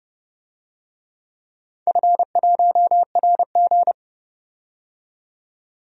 Morse code, heard as a keyed tone, about 30 wpm, 700 Hz. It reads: F1RG
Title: DXCC-100 Call Signs - 4 Characters - 30wpm